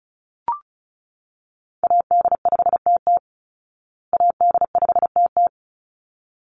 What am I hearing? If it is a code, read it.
Morse code, heard as a keyed tone, about 34 wpm, 700 Hz. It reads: AD5TT AD5TT
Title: Call Signs Repeated 3x - 35wpm